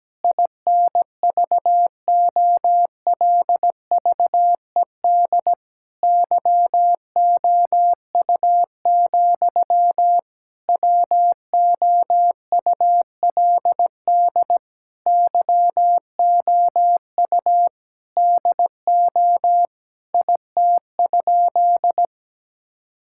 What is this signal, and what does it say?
Morse code, heard as a keyed tone, about 17 wpm, 700 Hz. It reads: INVOLVED YOU, WOULD YOU DO IT?
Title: Sentences from Top 1000 Words - Review Part 5 - 17wpm